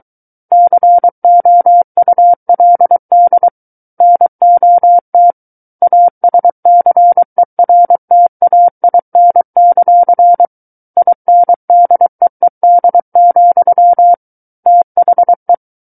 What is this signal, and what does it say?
Morse code, heard as a keyed tone, about 23 wpm, 700 Hz. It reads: COULD NOT ASCERTAIN; INDEED, THE
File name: Book One / 14-Chapter 14 - 23 WPM